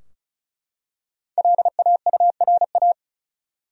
Morse code, heard as a keyed tone, about 35 wpm, 700 Hz. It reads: LAURA
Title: USA Common Names - Max 5 Letters - 35wpm